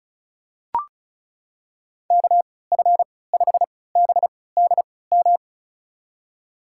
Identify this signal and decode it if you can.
Morse code, heard as a keyed tone, about 36 wpm, spaced as for 20 wpm, 700 Hz. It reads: KF5BDM
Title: USA Call Signs - 6 Characters - 3x Character Spacing - 35wpm